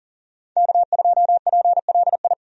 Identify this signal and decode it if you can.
Morse code, heard as a keyed tone, about 37 wpm, 700 Hz. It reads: K2PLI
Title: USA Call Signs - 5 Characters - 40wpm